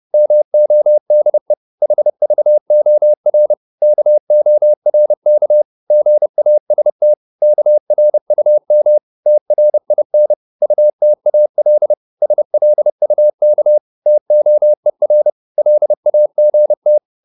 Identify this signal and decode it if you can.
Morse code, heard as a keyed tone, about 30 wpm, 600 Hz. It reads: MODE HVOR KORK GAST KRUM TRIN UTAL SLUK TOER LAGT
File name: Korte ord DK 30wpm